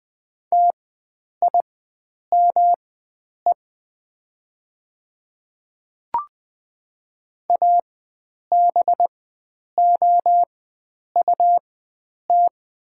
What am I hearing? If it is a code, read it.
Morse code, heard as a keyed tone, about 20 wpm, spaced as for 9 wpm, 700 Hz. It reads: TIME ABOUT
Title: 4x Character Spacing - Top 100 Words - 20wpm